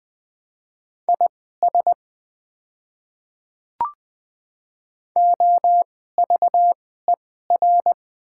Morse code, heard as a keyed tone, about 20 wpm, spaced as for 14 wpm, 700 Hz. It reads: IS OVER
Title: 2x Character Spacing - Top 100 Words - 20wpm